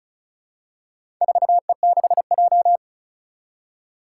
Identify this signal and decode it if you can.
Morse code, heard as a keyed tone, about 34 wpm, 700 Hz. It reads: 4E6J
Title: Four Challenging Letter-Number Combinations 35wpm